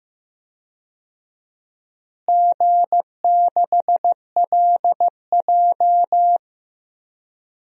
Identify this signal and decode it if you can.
Morse code, heard as a keyed tone, about 16 wpm, 700 Hz. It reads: G6LJ
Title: DXCC-100 Call Signs - 4 Characters - Repeated 3x - 15wpm